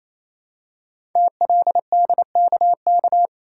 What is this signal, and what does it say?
Morse code, heard as a keyed tone, about 27 wpm, 700 Hz. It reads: TLDKK
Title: Code Groups - 5 Letters - 28wpm